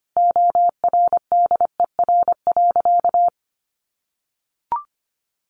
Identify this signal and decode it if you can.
Morse code, heard as a keyed tone, about 25 wpm, 700 Hz. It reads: ORDER.
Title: Sentences from Top 1300 Words 25wpm